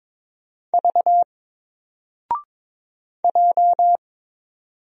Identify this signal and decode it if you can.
Morse code, heard as a keyed tone, about 23 wpm, 700 Hz. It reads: V J
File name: Single-Letter-Number-22wpm